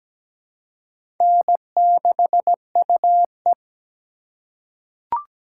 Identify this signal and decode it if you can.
Morse code, heard as a keyed tone, about 17 wpm, 700 Hz. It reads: N6UE
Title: USA Call Signs - 4 Characters - 4x Character Spacing - 17wpm